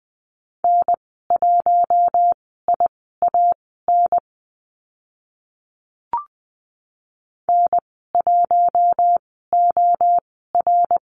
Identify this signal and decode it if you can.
Morse code, heard as a keyed tone, about 20 wpm, spaced as for 14 wpm, 700 Hz. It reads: N1IAN N1OR
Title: 062 Lesson 11g - Letter h - Call Signs 20wpm